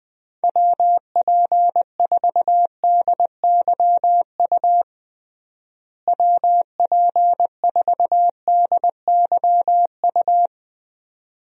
Morse code, heard as a keyed tone, about 20 wpm, 700 Hz. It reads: WP4DYU WP4DYU
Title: USA Call Signs - 6 Characters - Repeated 2x - 20wpm